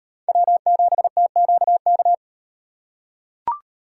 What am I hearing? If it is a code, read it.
Morse code, heard as a keyed tone, about 38 wpm, 700 Hz. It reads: W7TQK